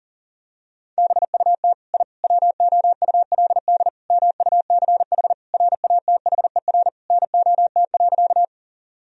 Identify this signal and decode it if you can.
Morse code, heard as a keyed tone, about 40 wpm, 700 Hz. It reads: BUT I WOULD MUCH RATHER NOT.
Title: Sentences from Top 1000 Words - Review Part 4 - 40wpm